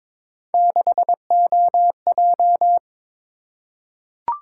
Morse code, heard as a keyed tone, about 22 wpm, 700 Hz. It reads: 6OJ
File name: practice / 22wpm / Three Letter-Number Combinations - 22wpm